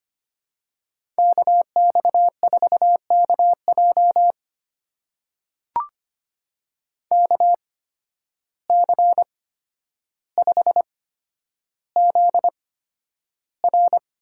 Morse code, heard as a keyed tone, about 25 wpm, spaced as for 11 wpm, 700 Hz. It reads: KX4KJ KC5ZR
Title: USA Call Signs - 5 Characters - 8x Character Spacing - 25wpm